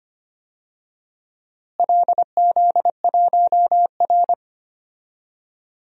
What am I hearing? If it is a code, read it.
Morse code, heard as a keyed tone, about 25 wpm, 700 Hz. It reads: LZ1R
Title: DXCC-100 Call Signs - 4 Characters - Repeated 2x - 25wpm